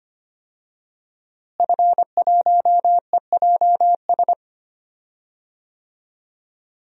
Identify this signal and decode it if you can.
Morse code, heard as a keyed tone, about 25 wpm, 700 Hz. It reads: F1EJS
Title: DXCC-100 Call Signs - 5 Characters - Repeated 3x - 25wpm